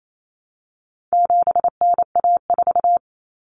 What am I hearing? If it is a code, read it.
Morse code, heard as a keyed tone, about 28 wpm, 700 Hz. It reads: 7NA4
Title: Common Cut Numbers - 4-Digit 28wpm